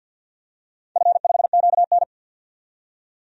Morse code, heard as a keyed tone, about 51 wpm, 700 Hz. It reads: F5QN
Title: DXCC-100 Call Signs - 4 Characters - 50wpm